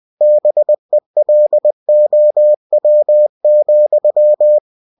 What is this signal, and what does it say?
Morse code, heard as a keyed tone, about 20 wpm, 600 Hz. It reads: BELOW,
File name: cw-books / winnie-the-pooh / winnie-20wpm-page-0003